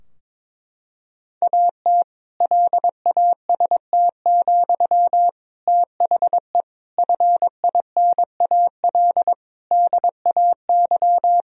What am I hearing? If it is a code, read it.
Morse code, heard as a keyed tone, about 22 wpm, 700 Hz. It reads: AT LAST, THE FINAL DAY
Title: Sentences from Top 1000 Words - Review Part 2 - 22wpm